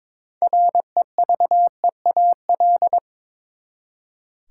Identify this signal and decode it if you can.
Morse code, heard as a keyed tone, about 21 wpm, 700 Hz. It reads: REVEAL
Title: Six Letter Words - 22wpm